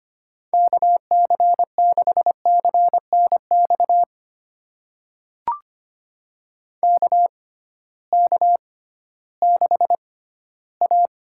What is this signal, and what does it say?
Morse code, heard as a keyed tone, about 25 wpm, spaced as for 14 wpm, 700 Hz. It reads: KC6CNX KK6A
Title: USA Call Signs - 6 Characters - 6x Character Spacing - 25wpm